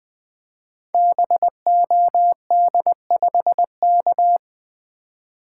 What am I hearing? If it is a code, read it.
Morse code, heard as a keyed tone, about 20 wpm, 700 Hz. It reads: BOD5K
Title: Code Groups - 5 Letter:Numbers - 20wpm